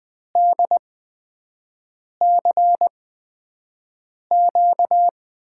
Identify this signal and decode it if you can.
Morse code, heard as a keyed tone, about 20 wpm, spaced as for 5 wpm, 700 Hz. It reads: DCQ
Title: USA Call Signs - 5 Characters - 8x Character Spacing - 20wpm